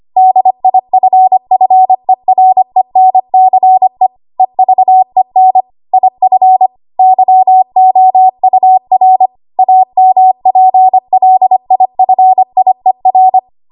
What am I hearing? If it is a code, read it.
Morse code, heard as a keyed tone, about 25 wpm, 750 Hz. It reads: DIFFERENCE EVEN IF YOUR AMPLIFIER
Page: W1AW Web Code Practice Files